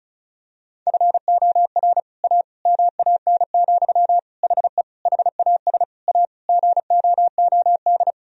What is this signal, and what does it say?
Morse code, heard as a keyed tone, about 34 wpm, 700 Hz. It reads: FOR A MAN, HE HAS A GOOD
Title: Sentences from Top 1000 Words - Review Part 2 - 35wpm